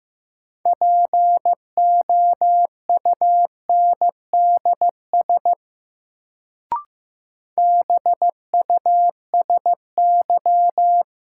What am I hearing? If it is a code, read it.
Morse code, heard as a keyed tone, about 15 wpm, 700 Hz. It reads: POUNDS BUSY
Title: Sentences from Top 1000 Words - Review Part 2 - New Word Drill - 15wpm